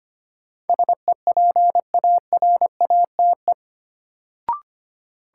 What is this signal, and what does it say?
Morse code, heard as a keyed tone, about 25 wpm, 700 Hz. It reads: SEPARATE
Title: Sentences from Top 1000 Words - Review Part 7 - New Word Drill - 25wpm